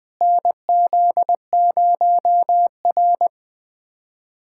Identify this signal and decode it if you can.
Morse code, heard as a keyed tone, about 21 wpm, 700 Hz. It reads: NZ0R
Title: USA Callsign DE Callsign - 4 Characters - 20wpm